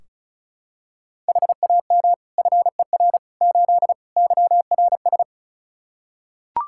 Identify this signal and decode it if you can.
Morse code, heard as a keyed tone, about 34 wpm, 700 Hz. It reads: HAM FER 8 YRS
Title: Practice QSO - Exchange 3 - More Info - 35wpm